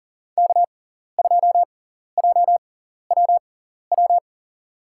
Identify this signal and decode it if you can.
Morse code, heard as a keyed tone, about 40 wpm, spaced as for 14 wpm, 700 Hz. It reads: K2JWW
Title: USA Call Signs - 5 Characters - 6x Character Spacing - 40wpm